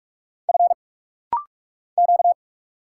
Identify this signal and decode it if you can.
Morse code, heard as a keyed tone, about 46 wpm, 700 Hz. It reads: F Q
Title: Single Letter-Number Alt - Rapid-Fire 45wpm